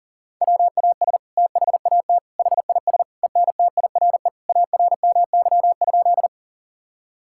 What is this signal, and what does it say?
Morse code, heard as a keyed tone, about 40 wpm, 690 Hz. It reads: WAS THAT HIS ENTIRE ARMY?
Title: Sentences from Top 1000 Words - Review Part 5 - 40wpm